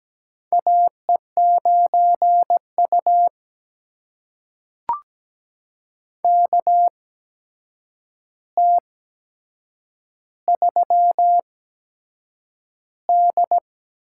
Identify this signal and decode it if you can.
Morse code, heard as a keyed tone, about 17 wpm, spaced as for 7 wpm, 700 Hz. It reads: AE9U KT3D
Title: USA Call Signs - 4 Characters - 8x Character Spacing - 17wpm